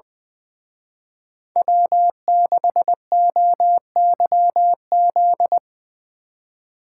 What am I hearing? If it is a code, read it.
Morse code, heard as a keyed tone, about 20 wpm, 700 Hz. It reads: W6OYZ
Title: Speed Racing - USA 1x3 Call Signs 20wpm